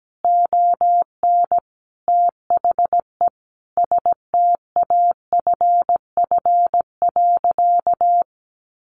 Morse code, heard as a keyed tone, about 17 wpm, 700 Hz. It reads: ON THE STAFF.
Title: Sentences from Top 1200 Words 17wpm